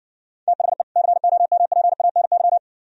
Morse code, heard as a keyed tone, about 70 wpm, 700 Hz. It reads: THE COMPANY